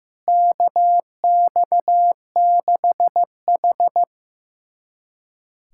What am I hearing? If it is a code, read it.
Morse code, heard as a keyed tone, about 15 wpm, 700 Hz. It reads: KX6H
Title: USA Callsign DE Callsign - 4 Characters - 15wpm